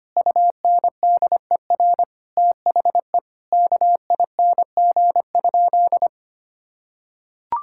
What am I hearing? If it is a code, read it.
Morse code, heard as a keyed tone, about 25 wpm, 700 Hz. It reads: UNDER THE KING?
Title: Sentences from Top 1000 Words - Review Part 6 - 25wpm